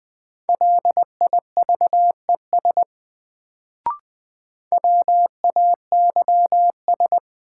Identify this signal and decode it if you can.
Morse code, heard as a keyed tone, about 21 wpm, 700 Hz. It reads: LIVES WAYS
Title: Sets of 5 Words Encore - New Word Drill - 20wpm